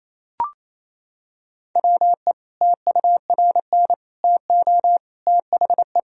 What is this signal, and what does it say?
Morse code, heard as a keyed tone, about 28 wpm, 700 Hz. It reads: WE TURN TO THE
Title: Sentences from Top 1000 Words - Review Part 1 - 28wpm